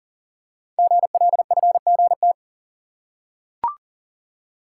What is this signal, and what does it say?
Morse code, heard as a keyed tone, about 40 wpm, 700 Hz. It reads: GLFGT
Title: Code Groups - 5 Letters - 40wpm